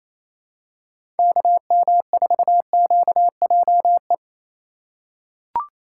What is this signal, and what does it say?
Morse code, heard as a keyed tone, about 28 wpm, 700 Hz. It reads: KM4QJE
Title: USA Call Signs - 3x Character Spacing - 28wpm